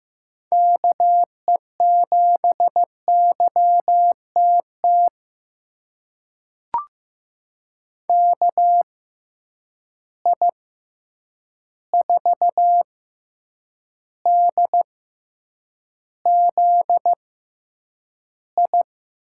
Morse code, heard as a keyed tone, about 15 wpm, spaced as for 8 wpm, 700 Hz. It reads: KE7YTT KI4DZI
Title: USA Call Signs - 6 Characters - 6x Character Spacing - 15wpm